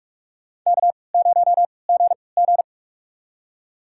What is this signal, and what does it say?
Morse code, heard as a keyed tone, about 46 wpm, spaced as for 25 wpm, 700 Hz. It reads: K0GG
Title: USA Call Signs - 4 Characters - 3x Character Spacing - 45wpm